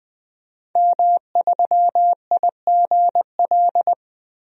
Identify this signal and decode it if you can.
Morse code, heard as a keyed tone, about 20 wpm, 700 Hz. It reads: M3IGL
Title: DXCC-100 Call Signs - 5 Characters - Repeated 2x - 20wpm